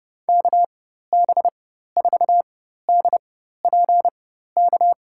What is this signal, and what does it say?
Morse code, heard as a keyed tone, about 31 wpm, spaced as for 14 wpm, 700 Hz. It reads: KB4DPK
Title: USA Call Signs - 4x Character Spacing - 30wpm